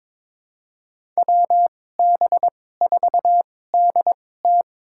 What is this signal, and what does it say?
Morse code, heard as a keyed tone, about 22 wpm, spaced as for 16 wpm, 700 Hz. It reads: WB4DT
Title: USA Call Signs - 5 Characters - 2x Character Spacing - 22wpm